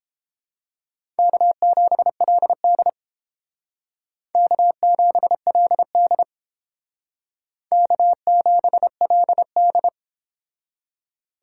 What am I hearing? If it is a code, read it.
Morse code, heard as a keyed tone, about 30 wpm, 700 Hz. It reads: K7LD K7LD K7LD